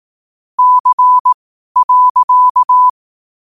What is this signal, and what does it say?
Morse code, heard as a keyed tone, about 18 wpm, spaced as for 12 wpm, 1000 Hz. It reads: C.
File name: qotdfiles / 12wpm